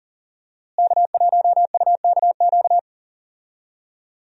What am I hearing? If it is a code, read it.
Morse code, heard as a keyed tone, about 37 wpm, 700 Hz. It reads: K1UKQ
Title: USA Call Signs - 5 Characters - Repeated 2x - 40wpm